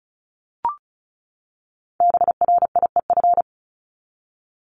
Morse code, heard as a keyed tone, about 34 wpm, 700 Hz. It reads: BRIEF
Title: Top 1501-1750 Words 35wpm